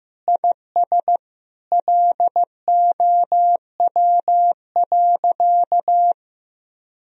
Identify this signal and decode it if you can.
Morse code, heard as a keyed tone, about 15 wpm, 700 Hz. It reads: IS LOW.
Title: Sentences from Top 1000 Words - Review Part 1 - 15wpm